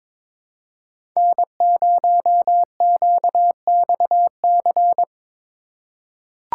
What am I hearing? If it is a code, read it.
Morse code, heard as a keyed tone, about 22 wpm, 700 Hz. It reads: N0QXC
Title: USA Call Signs - 5 Characters - 4x Character Spacing - 22wpm